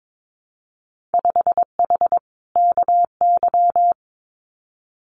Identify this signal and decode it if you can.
Morse code, heard as a keyed tone, about 22 wpm, 700 Hz. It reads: 5H KY
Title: Winter Field Day Exchanges Encore 22wpm